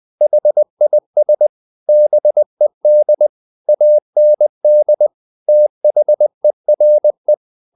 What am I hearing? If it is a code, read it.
Morse code, heard as a keyed tone, about 20 wpm, 600 Hz. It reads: HIS BED AND THERE
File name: the old man and the sea part 2 20wpm